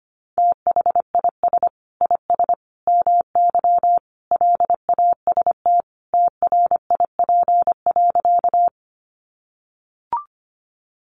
Morse code, heard as a keyed tone, about 25 wpm, 700 Hz. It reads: THIS IS MY LAST TRIP.
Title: Sentences from Top 1300 Words 25wpm